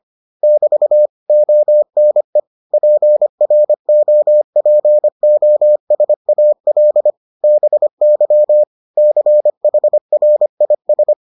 Morse code, heard as a keyed tone, about 25 wpm, 600 Hz. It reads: <BT> ONE PROPOSAL BY CHRIS